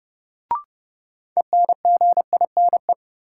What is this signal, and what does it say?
Morse code, heard as a keyed tone, about 32 wpm, 700 Hz. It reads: ENGINE
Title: Sentences from Top 1500 Words - New Word Drill - 30wpm